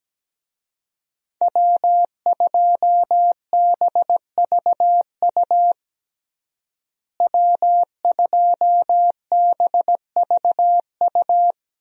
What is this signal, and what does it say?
Morse code, heard as a keyed tone, about 17 wpm, 700 Hz. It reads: W2BVU W2BVU
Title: USA Call Signs - 5 Characters - Repeated 2x - 17wpm